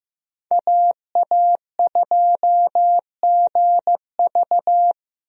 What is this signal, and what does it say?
Morse code, heard as a keyed tone, about 14 wpm, 700 Hz. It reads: AA2GV
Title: USA Call Signs - 5 Characters - Repeated 2x - 15wpm